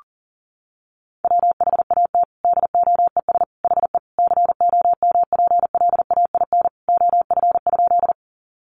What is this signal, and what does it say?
Morse code, heard as a keyed tone, about 40 wpm, 700 Hz. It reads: WHAT DOES HE COMPLAIN OF?
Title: Sentences from Top 1300 Words 40wpm